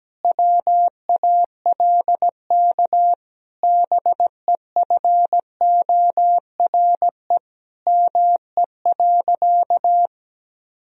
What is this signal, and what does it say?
Morse code, heard as a keyed tone, about 17 wpm, 700 Hz. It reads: WALK BEFORE ME.
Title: Sentences from Top 1000 Words - Review Part 1 - 17wpm